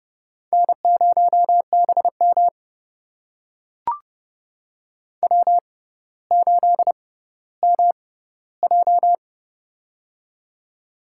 Morse code, heard as a keyed tone, about 30 wpm, spaced as for 15 wpm, 700 Hz. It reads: N0BM W8MJ
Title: USA Call Signs - 6x Character Spacing - 30wpm